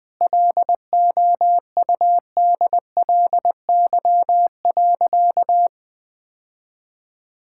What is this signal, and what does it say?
Morse code, heard as a keyed tone, about 20 wpm, 700 Hz. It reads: LOUDLY.
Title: Sentences with 2-Word Phrases - Spoken Beginning - 20wpm